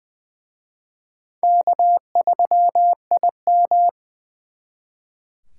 Morse code, heard as a keyed tone, about 24 wpm, 700 Hz. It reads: K3IM
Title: Speed Racing - USA 1x2 Call Signs 20wpm